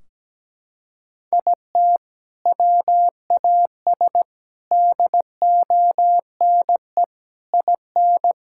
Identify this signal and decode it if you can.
Morse code, heard as a keyed tone, about 17 wpm, 700 Hz. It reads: IT WAS DONE IN
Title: Sentences from Top 1600 Words 17wpm